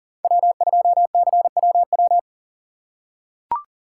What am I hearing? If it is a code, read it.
Morse code, heard as a keyed tone, about 40 wpm, 700 Hz. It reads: W2CWW
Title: USA Call Signs - 5 Characters - 3x Character Spacing - 40wpm